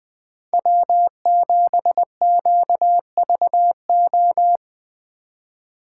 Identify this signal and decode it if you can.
Morse code, heard as a keyed tone, about 23 wpm, 700 Hz. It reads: W7QVO
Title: Speed Racing - USA 1x3 Call Signs 20wpm